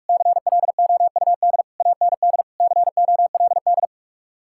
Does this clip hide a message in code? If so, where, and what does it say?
Yes: Morse code from 0.1 to 3.9 s, heard as a keyed tone, about 59 wpm, 700 Hz. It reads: CLOUD AND COLD